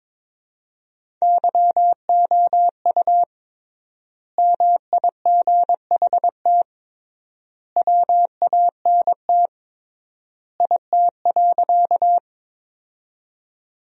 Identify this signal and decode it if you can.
Morse code, heard as a keyed tone, about 22 wpm, 700 Hz. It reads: YOU MIGHT WANT IT.
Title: 3x Word-Spacing - Sentences from Top 300 Words - 22wpm